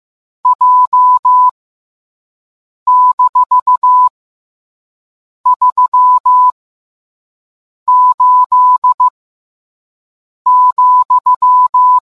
Morse code, heard as a keyed tone, about 15 wpm, spaced as for 5 wpm, 1000 Hz. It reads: J-38,